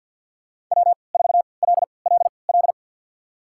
Morse code, heard as a keyed tone, about 53 wpm, spaced as for 29 wpm, 700 Hz. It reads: W4LLL
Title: USA Call Signs - 5 Characters - 3x Character Spacing - 50wpm